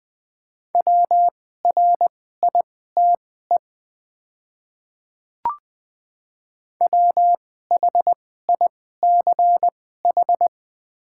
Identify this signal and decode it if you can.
Morse code, heard as a keyed tone, about 20 wpm, spaced as for 14 wpm, 700 Hz. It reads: WRITE WHICH
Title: 2x Character Spacing - Top 100 Words - 20wpm